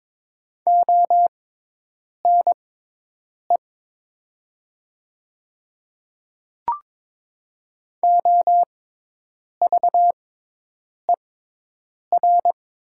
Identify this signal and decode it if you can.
Morse code, heard as a keyed tone, about 22 wpm, spaced as for 8 wpm, 700 Hz. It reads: ONE OVER
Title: 6x Character Spacing - Top 300 Words - 22wpm